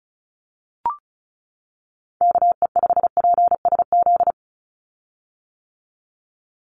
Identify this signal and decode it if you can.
Morse code, heard as a keyed tone, about 34 wpm, 700 Hz. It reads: KE5PSZ
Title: USA Call Signs 35wpm